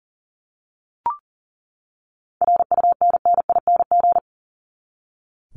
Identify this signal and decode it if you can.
Morse code, heard as a keyed tone, about 40 wpm, 700 Hz. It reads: RUNNING
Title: ING Suffix - Top 100 Words with Base Words 40wpm